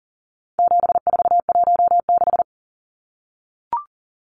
Morse code, heard as a keyed tone, about 40 wpm, 700 Hz. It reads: 7416
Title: Numbers - 4-Digit 40wpm